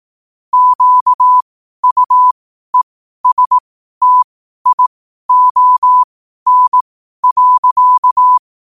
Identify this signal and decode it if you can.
Morse code, heard as a keyed tone, about 18 wpm, spaced as for 12 wpm, 1000 Hz. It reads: QUESTION.
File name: qotdfiles / 12wpm